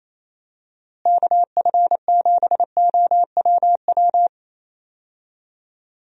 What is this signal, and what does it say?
Morse code, heard as a keyed tone, about 27 wpm, 700 Hz. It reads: KF7OWW
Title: USA Call Signs - 6 Characters - Repeated 3x - 28wpm